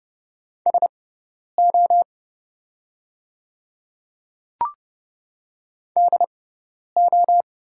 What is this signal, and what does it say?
Morse code, heard as a keyed tone, about 30 wpm, spaced as for 10 wpm, 700 Hz. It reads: SO DO
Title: 6x Character Spacing - Top 100 Words - 30wpm